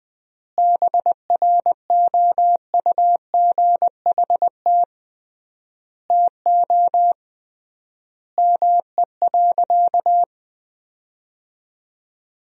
Morse code, heard as a keyed tone, about 21 wpm, 700 Hz. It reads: BROUGHT TO ME.
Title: 3x Word-Spacing - Sentences from Top 500 Words - 20wpm